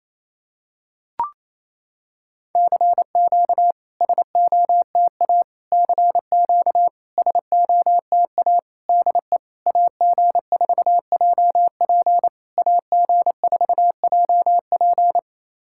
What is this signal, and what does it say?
Morse code, heard as a keyed tone, about 27 wpm, 690 Hz. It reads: CQ SOTA CQ SOTA DE AG4JP AG4JP
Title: SOTA QSO Practice 28wpm